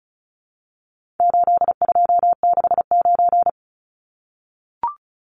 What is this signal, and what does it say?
Morse code, heard as a keyed tone, about 34 wpm, 700 Hz. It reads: 8269
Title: Numbers - 4-Digit 35wpm